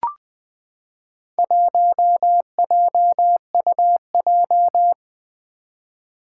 Morse code, heard as a keyed tone, about 20 wpm, 700 Hz. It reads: 1JUJ
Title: Four Challenging Letter-Number Combinations 20wpm